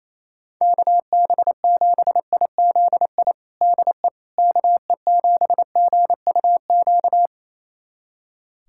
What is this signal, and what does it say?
Morse code, heard as a keyed tone, about 28 wpm, 700 Hz. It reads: KB7IZI DE KE7GUQ
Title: USA Callsign DE Callsign - 6 Characters - 28wpm